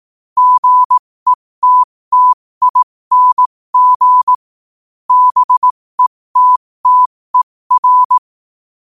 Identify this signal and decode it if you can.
Morse code, heard as a keyed tone, about 18 wpm, spaced as for 15 wpm, 1000 Hz. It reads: GETTING BETTER
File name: qotdfiles / 15wpm